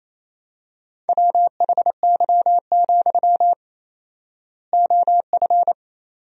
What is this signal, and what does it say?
Morse code, heard as a keyed tone, about 28 wpm, 700 Hz. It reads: WHY, OF
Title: 4x Word-Spacing - Sentences from Top 500 Words - 28wpm